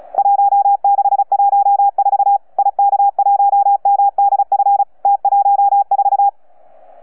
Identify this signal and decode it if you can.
Morse code, heard as a keyed tone, about 35 wpm, 800 Hz. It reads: 1614 IK1MDF T14
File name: UP2L-20-CW